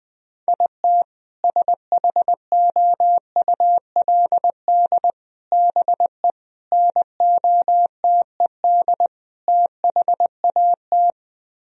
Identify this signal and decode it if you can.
Morse code, heard as a keyed tone, about 20 wpm, 700 Hz. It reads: IT SHOULD BE NOTED THAT
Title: Speed-Racing - Sets of 5 Words 20wpm